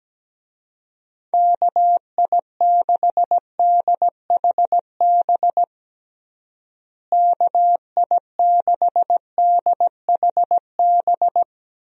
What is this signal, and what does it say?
Morse code, heard as a keyed tone, about 17 wpm, 700 Hz. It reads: KI6DHB KI6DHB
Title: USA Call Signs - 6 Characters - Repeated 3x - 17wpm